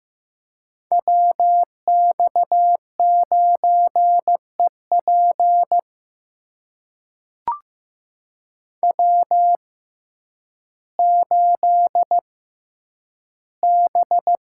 Morse code, heard as a keyed tone, about 15 wpm, spaced as for 9 wpm, 700 Hz. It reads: WX9EP W8B
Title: USA Call Signs - 5 Characters - 6x Character Spacing - 15wpm